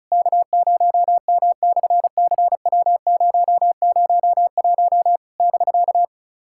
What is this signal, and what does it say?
Morse code, heard as a keyed tone, about 36 wpm, 700 Hz. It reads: K0M/CW001 <BK>
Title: SOTA QSO Practice 35wpm